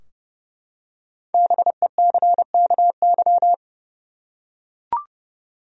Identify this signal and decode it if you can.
Morse code, heard as a keyed tone, about 30 wpm, 700 Hz. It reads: BECKY
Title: USA Common Names - 30wpm